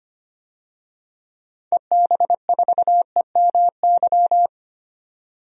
Morse code, heard as a keyed tone, about 25 wpm, 700 Hz. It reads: EB4EMY
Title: DXCC-100 Call Signs - 6 Characters - 25wpm